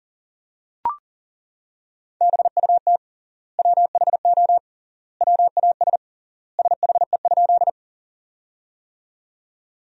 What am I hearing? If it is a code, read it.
Morse code, heard as a keyed tone, about 37 wpm, 700 Hz. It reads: BUT WHO WAS SHE?
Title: 3x Word-Spacing - Sentences from Top 100 Words - 40wpm